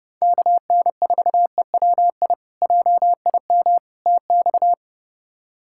Morse code, heard as a keyed tone, about 30 wpm, 700 Hz. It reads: KN4EWI JIM TX
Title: K1USN SST QSO Practice - 30wpm